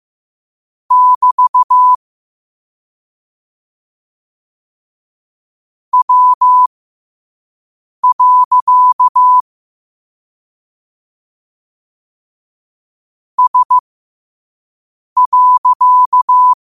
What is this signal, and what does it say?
Morse code, heard as a keyed tone, about 15 wpm, spaced as for 5 wpm, 1000 Hz. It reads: <BT> W. S.
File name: qotdfiles / 5wpm